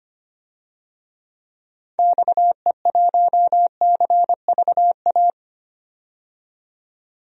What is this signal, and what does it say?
Morse code, heard as a keyed tone, about 25 wpm, 700 Hz. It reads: XE1CVA
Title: DXCC-100 Call Signs - 6 Characters - 25wpm